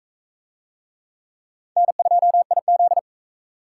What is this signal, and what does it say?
Morse code, heard as a keyed tone, about 39 wpm, 700 Hz. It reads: N2IZ